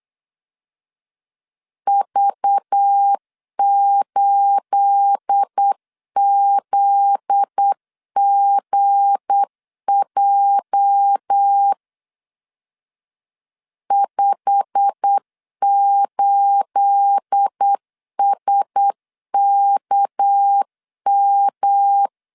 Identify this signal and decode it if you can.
Morse code, heard as a keyed tone, about 8 wpm, 800 Hz. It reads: V8ZGJ 58SKM
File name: CW / DET / 1 9 1500C S B 8wpm 5624